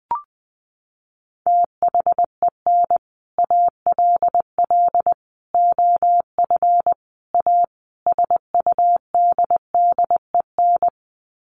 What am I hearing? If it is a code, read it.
Morse code, heard as a keyed tone, about 20 wpm, 700 Hz. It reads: THEN ALL OF A SUDDEN
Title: Sets of 5 Words Encore 20wpm